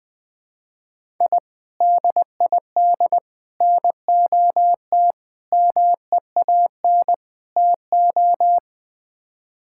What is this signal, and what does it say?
Morse code, heard as a keyed tone, about 20 wpm, 700 Hz. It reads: I DID NOT MEAN TO
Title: Build A Phrase - 5-Word Phrases - 20wpm